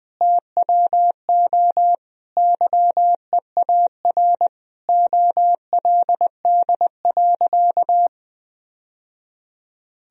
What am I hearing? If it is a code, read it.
Morse code, heard as a keyed tone, about 20 wpm, 700 Hz. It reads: TWO YEAR OLD.
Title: Sentences from Top 1000 Words - Review Part 1 - 20wpm